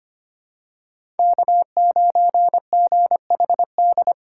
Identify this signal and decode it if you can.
Morse code, heard as a keyed tone, about 25 wpm, 700 Hz. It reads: K9GHD
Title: USA Call Signs - 5 Characters - 6x Character Spacing - 25wpm